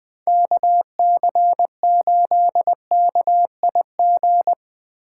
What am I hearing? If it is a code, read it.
Morse code, heard as a keyed tone, about 22 wpm, 700 Hz. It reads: KC8KIG
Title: Speed Racing - USA 2x3 Call Signs 20wpm